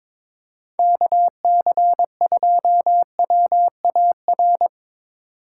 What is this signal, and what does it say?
Morse code, heard as a keyed tone, about 23 wpm, 700 Hz. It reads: KC2WAR
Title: USA Call Signs - 4x Character Spacing - 22wpm